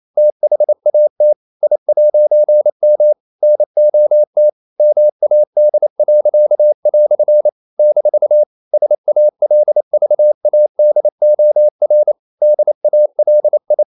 Morse code, heard as a keyed tone, about 28 wpm, 600 Hz. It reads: THAT I'M NOT MAD." - SALVADOR DALI
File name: CW / Practice / EnglishQuotations0128